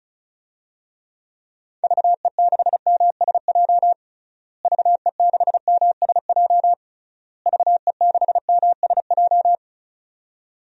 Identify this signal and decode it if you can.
Morse code, heard as a keyed tone, about 36 wpm, 700 Hz. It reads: VE6MSJ VE6MSJ VE6MSJ
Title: DXCC-100 Call Signs - 6 Characters - Repeated 3x - 35wpm